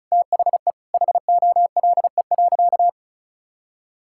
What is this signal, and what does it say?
Morse code, heard as a keyed tone, about 34 wpm, 700 Hz. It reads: THE HOLE.
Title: Sentences from Top 1600 Words 35wpm